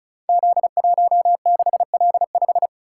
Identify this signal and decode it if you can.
Morse code, heard as a keyed tone, about 36 wpm, 700 Hz. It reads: Z16L5
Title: Code Groups - 5 Letter:Numbers - 35wpm